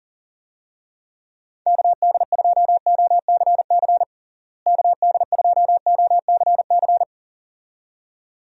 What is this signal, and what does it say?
Morse code, heard as a keyed tone, about 37 wpm, 700 Hz. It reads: KD2OCC KD2OCC
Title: USA Call Signs - 6 Characters - Repeated 2x - 40wpm